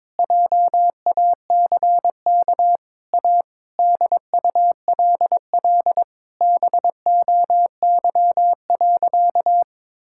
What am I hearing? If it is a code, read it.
Morse code, heard as a keyed tone, about 22 wpm, 700 Hz. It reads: JACK A DULL BOY.
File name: Sentences with 7-Word Phrases - 22wpm